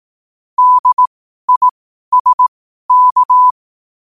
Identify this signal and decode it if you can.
Morse code, heard as a keyed tone, about 18 wpm, spaced as for 12 wpm, 1000 Hz. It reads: DISK
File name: qotdfiles / 12wpm